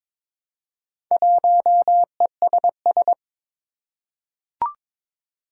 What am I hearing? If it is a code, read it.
Morse code, heard as a keyed tone, about 23 wpm, 700 Hz. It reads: 1ESS
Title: Four Challenging Letter-Number Combinations 22wpm